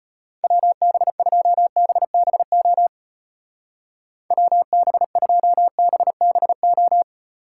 Morse code, heard as a keyed tone, about 31 wpm, 700 Hz. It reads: WB2BBO WB2BBO